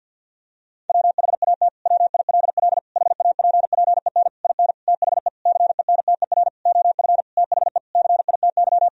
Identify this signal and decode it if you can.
Morse code, heard as a keyed tone, about 52 wpm, 700 Hz. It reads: WHAT WILL HAPPEN IN THE CENTER OF THE CITY
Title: Sentences from Top 1000 Words - Review Part 2 - 50wpm